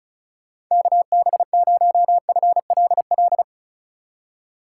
Morse code, heard as a keyed tone, about 43 wpm, 700 Hz. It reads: KB0FLL
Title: Speed Racing - USA 2x3 Call Signs 35wpm